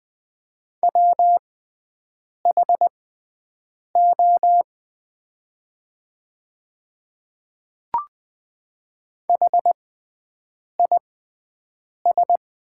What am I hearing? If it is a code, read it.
Morse code, heard as a keyed tone, about 20 wpm, spaced as for 7 wpm, 700 Hz. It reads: WHO HIS
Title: 6x Character Spacing - Top 100 Words - 20wpm